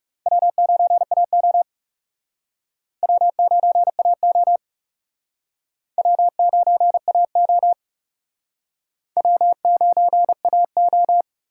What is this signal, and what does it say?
Morse code, heard as a keyed tone, about 36 wpm, 700 Hz. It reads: W9AO W9AO W9AO W9AO
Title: Speed Racing - USA 1x2 Call Signs 30wpm